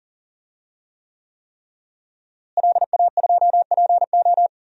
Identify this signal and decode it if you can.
Morse code, heard as a keyed tone, about 40 wpm, 700 Hz. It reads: LA2PO
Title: DXCC-100 Call Signs - 5 Characters - Repeated 3x - 40wpm